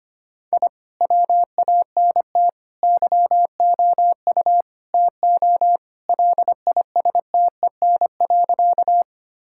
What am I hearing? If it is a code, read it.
Morse code, heard as a keyed tone, about 25 wpm, 700 Hz. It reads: I WANT YOU TO LISTEN.
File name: practice / 25wpm / Sentences with 4-Word Phrases - 25wpm